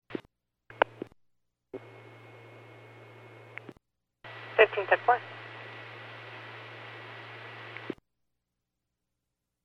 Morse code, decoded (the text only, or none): none